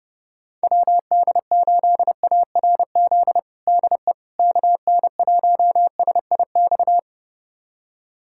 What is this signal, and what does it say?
Morse code, heard as a keyed tone, about 32 wpm, 700 Hz. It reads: WD8ARZ DE KN1SIX
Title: USA Callsign DE Callsign - 6 Characters - 30wpm